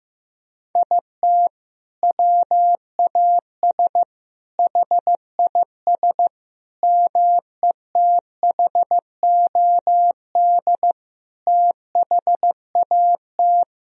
Morse code, heard as a keyed tone, about 15 wpm, 700 Hz. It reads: IT WAS HIS METHOD THAT
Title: Sentences from Top 1000 Words - Review Part 3 - 15wpm